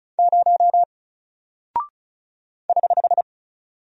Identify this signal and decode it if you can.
Morse code, heard as a keyed tone, about 36 wpm, 700 Hz. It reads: 0 <HH>
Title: Single Character (Letter:Number:Punctuation:Prosign) Alt 35wpm